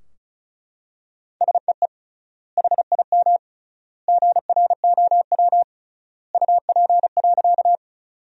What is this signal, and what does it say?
Morse code, heard as a keyed tone, about 36 wpm, 700 Hz. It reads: SEE HIM GROW UP.
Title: 3x Word-Spacing - Sentences from Top 300 Words - 35wpm